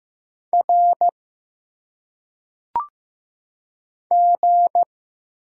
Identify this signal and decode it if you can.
Morse code, heard as a keyed tone, about 15 wpm, 700 Hz. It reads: R G
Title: Single Letters - Phonetic Pronunciation 15wpm